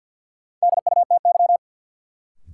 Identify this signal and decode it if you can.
Morse code, heard as a keyed tone, about 52 wpm, 700 Hz. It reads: DUTY
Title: Sentences from Top 1500 Words - New Word Drill - 50wpm